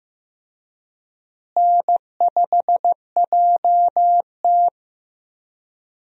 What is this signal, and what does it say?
Morse code, heard as a keyed tone, about 15 wpm, 700 Hz. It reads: N5JT
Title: USA Call Signs - 4 Characters - Repeated 3x - 15wpm